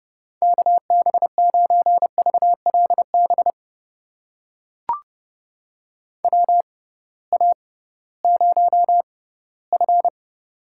Morse code, heard as a keyed tone, about 30 wpm, spaced as for 18 wpm, 700 Hz. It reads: KB9VLB WA0F
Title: USA Call Signs - 6x Character Spacing - 30wpm